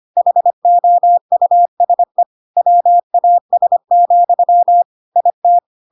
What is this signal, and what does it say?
Morse code, heard as a keyed tone, about 25 wpm, 700 Hz. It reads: HOUSE WAS, IT
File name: Pinocchio / Pinocchio ch3 25 25 ik1hge